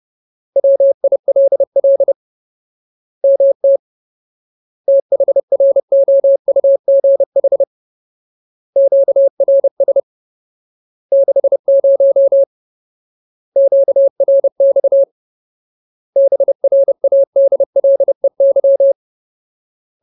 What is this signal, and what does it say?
Morse code, heard as a keyed tone, about 30 wpm, 550 Hz. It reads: WILL MT THROUGH QRS 60 QRX BRADLEY
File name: Ultimate Word List / 30WPM Ultimate